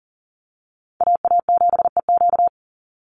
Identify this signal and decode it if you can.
Morse code, heard as a keyed tone, about 40 wpm, 700 Hz. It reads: AA7EQ
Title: NA SOTA Chaser Callsigns 40wpm